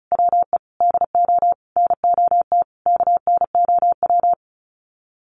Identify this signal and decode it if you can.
Morse code, heard as a keyed tone, about 34 wpm, 700 Hz. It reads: WE DO NOT KNOW
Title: Sets of 5 Words Encore 35wpm